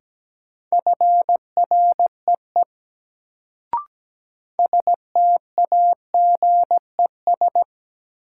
Morse code, heard as a keyed tone, about 17 wpm, 700 Hz. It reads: FREE STAGES
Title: Sets of 5 Words Encore - New Word Drill - 17wpm